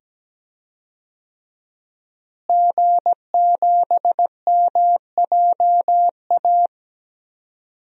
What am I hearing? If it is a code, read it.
Morse code, heard as a keyed tone, about 17 wpm, 700 Hz. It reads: G7MJA
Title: DXCC-100 Call Signs - 5 Characters - Repeated 2x - 17wpm